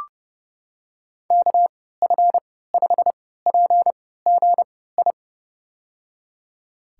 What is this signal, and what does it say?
Morse code, heard as a keyed tone, about 30 wpm, spaced as for 17 wpm, 700 Hz. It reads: KF5PGI
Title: USA Call Signs - 6 Characters - 3x Character Spacing - 30wpm